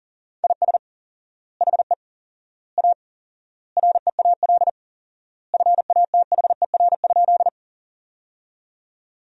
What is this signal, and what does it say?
Morse code, heard as a keyed tone, about 40 wpm, 700 Hz. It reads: IS HE A REAL FATHER?
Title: 4x Word-Spacing - Sentences from Top 300 Words - 40wpm